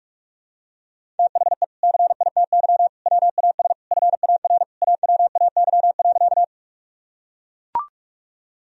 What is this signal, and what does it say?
Morse code, heard as a keyed tone, about 44 wpm, 700 Hz. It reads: THE CITY WAS FAR AWAY.
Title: Sentences from Top 1000 Words - Review Part 1 - 45wpm